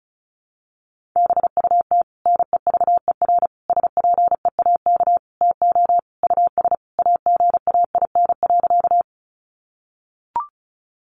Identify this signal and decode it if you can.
Morse code, heard as a keyed tone, about 34 wpm, 700 Hz. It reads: BUT NEVER SPEAK TO US AGAIN.
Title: Sentences from Top 1100 Words 35wpm